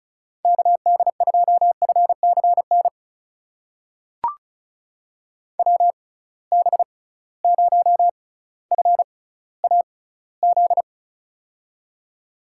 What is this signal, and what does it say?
Morse code, heard as a keyed tone, about 36 wpm, spaced as for 18 wpm, 700 Hz. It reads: KD2FCN WB0FAZ
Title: USA Call Signs - 6 Characters - 6x Character Spacing - 35wpm